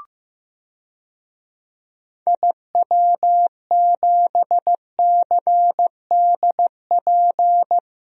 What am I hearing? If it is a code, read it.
Morse code, heard as a keyed tone, about 16 wpm, 700 Hz. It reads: IW7CDP
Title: DXCC-100 Call Signs - 6 Characters - 15wpm